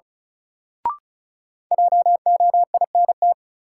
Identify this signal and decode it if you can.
Morse code, heard as a keyed tone, about 36 wpm, 700 Hz. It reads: JOINT
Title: Sentences from Top 1300 Words - New Word Drill Alt - 35wpm